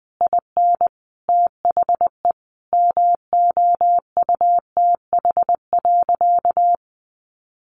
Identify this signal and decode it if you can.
Morse code, heard as a keyed tone, about 20 wpm, 700 Hz. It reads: IN THE MOUTH.
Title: Sentences from Top 1500 Words 20wpm